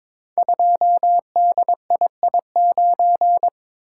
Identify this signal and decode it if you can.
Morse code, heard as a keyed tone, about 23 wpm, 700 Hz. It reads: 2DII9
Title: Code Groups - 5 Letter:Numbers - 22wpm